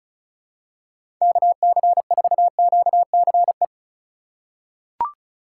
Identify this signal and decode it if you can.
Morse code, heard as a keyed tone, about 34 wpm, 700 Hz. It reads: KC4QCE
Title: USA Call Signs - 6 Characters - 2x Character Spacing - 35wpm